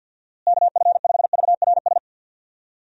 Morse code, heard as a keyed tone, about 46 wpm, 700 Hz. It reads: KF5VRS
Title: USA Call Signs - 6 Characters - Repeated 2x - 50wpm